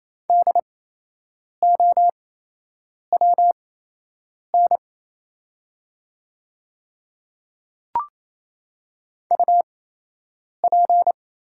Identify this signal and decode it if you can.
Morse code, heard as a keyed tone, about 27 wpm, spaced as for 8 wpm, 700 Hz. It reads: DOWN UP
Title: 8x Character Spacing - Top 100 Words - 28wpm